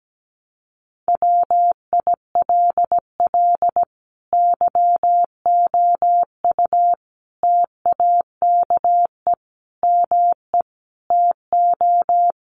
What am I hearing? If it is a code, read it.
Morse code, heard as a keyed tone, about 17 wpm, 700 Hz. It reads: WILL YOU TAKE ME TO
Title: Sentences from Top 200 Words 17wpm